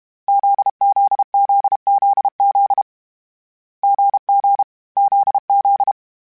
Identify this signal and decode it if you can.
Morse code, heard as a keyed tone, about 32 wpm, 800 Hz. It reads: ZZZZZ GG ZZ
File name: Difficult Combos / Difficult Combos-mp3s / GZ GZ7 @30